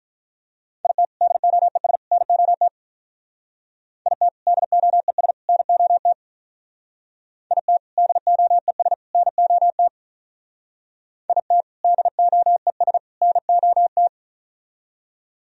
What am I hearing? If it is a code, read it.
Morse code, heard as a keyed tone, about 44 wpm, 700 Hz. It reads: IT DOES NOT IT DOES NOT IT DOES NOT IT DOES NOT
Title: Speed-Racing - Sets of 3 Words 35wpm